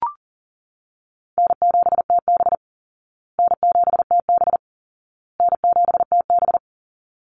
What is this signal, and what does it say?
Morse code, heard as a keyed tone, about 40 wpm, 700 Hz. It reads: N7TB N7TB N7TB
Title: Call Signs Repeated 2x - 40wpm